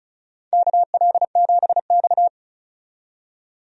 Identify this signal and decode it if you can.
Morse code, heard as a keyed tone, about 35 wpm, 700 Hz. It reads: KL7X
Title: DXCC-100 Call Signs - 4 Characters - 35wpm